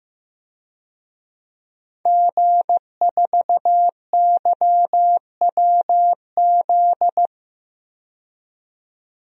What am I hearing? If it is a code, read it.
Morse code, heard as a keyed tone, about 15 wpm, 700 Hz. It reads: G4YWZ
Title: DXCC-100 Call Signs - 5 Characters - 15wpm